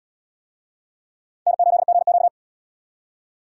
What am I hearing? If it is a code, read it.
Morse code, heard as a keyed tone, about 65 wpm, 690 Hz. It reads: N8GQ